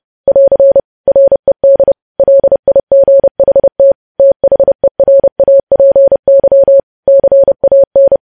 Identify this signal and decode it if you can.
Morse code, heard as a keyed tone, about 30 wpm, 550 Hz. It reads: <AR> RED LIGHT THERAPY CAN